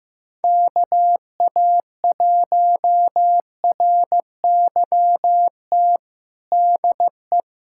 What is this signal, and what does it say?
Morse code, heard as a keyed tone, about 15 wpm, 700 Hz. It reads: KA1RYT DE
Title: USA Callsign DE Callsign - 6 Characters - 15wpm